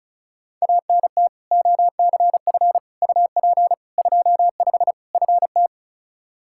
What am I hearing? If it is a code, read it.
Morse code, heard as a keyed tone, about 34 wpm, 700 Hz. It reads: ANT OCF UP 25 FT
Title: Practice QSO - Exchange 2 - Info - 35wpm